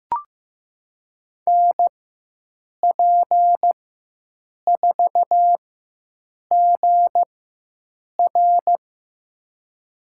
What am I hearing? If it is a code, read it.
Morse code, heard as a keyed tone, about 15 wpm, spaced as for 7 wpm, 700 Hz. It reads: NP4GR
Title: USA Call Signs - 5 Characters - 4x Character Spacing - 15wpm